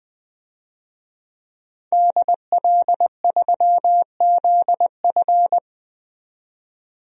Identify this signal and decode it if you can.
Morse code, heard as a keyed tone, about 20 wpm, 700 Hz. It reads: DL3ZF
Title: DXCC-100 Call Signs - 5 Characters - 20wpm